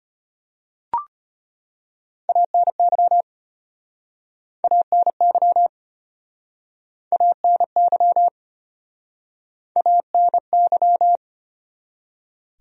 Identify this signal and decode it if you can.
Morse code, heard as a keyed tone, about 35 wpm, 700 Hz. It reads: ANY ANY ANY ANY